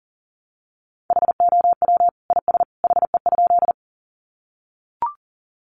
Morse code, heard as a keyed tone, about 40 wpm, 700 Hz. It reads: HOW IS HE?
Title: Sentences from Top 100 Words 40wpm